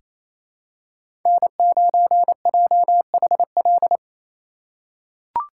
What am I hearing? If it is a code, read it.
Morse code, heard as a keyed tone, about 27 wpm, 700 Hz. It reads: N9JHL
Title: K1USN SST QSO Practice - 28wpm